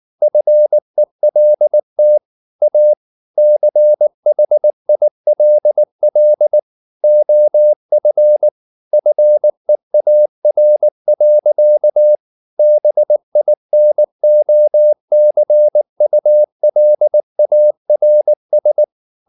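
Morse code, heard as a keyed tone, about 19 wpm, 600 Hz. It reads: FELT A CHILL OF FEAR. BINOCULARS
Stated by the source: Morse code at 19 wpm from CW med Gnister, reading Ken Follett's novel Never in English.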